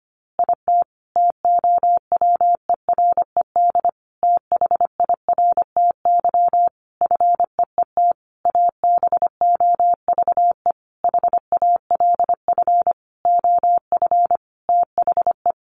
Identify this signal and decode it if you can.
Morse code, heard as a keyed tone, about 25 wpm, 700 Hz. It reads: IT TOWERED THIRTY FEET ABOVE HALF OF THE
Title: A Sound Of Thunder 25wpm